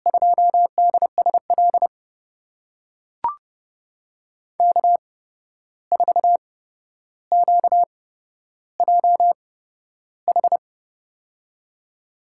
Morse code, heard as a keyed tone, about 30 wpm, spaced as for 12 wpm, 700 Hz. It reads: 2DSL K4QJH
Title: USA Call Signs - 5 Characters - 8x Character Spacing - 30wpm